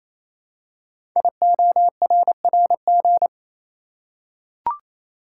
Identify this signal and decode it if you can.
Morse code, heard as a keyed tone, about 29 wpm, 700 Hz. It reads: IORRG
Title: Code Groups - 5 Letters - 28wpm